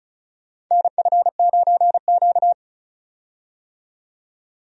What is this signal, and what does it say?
Morse code, heard as a keyed tone, about 36 wpm, 700 Hz. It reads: NF9Q
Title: USA Call Signs - 4 Characters - 35wpm